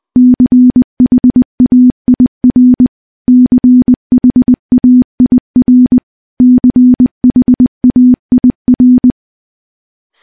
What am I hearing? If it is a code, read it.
Morse code, heard as a keyed tone, about 20 wpm, 250 Hz. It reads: CHAIR CHAIR CHAIR
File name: morse / morse-training-20WPM